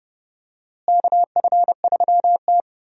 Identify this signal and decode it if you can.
Morse code, heard as a keyed tone, about 30 wpm, 700 Hz. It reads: KF3T
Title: USA Call Signs - 4 Characters - Repeated 3x - 30wpm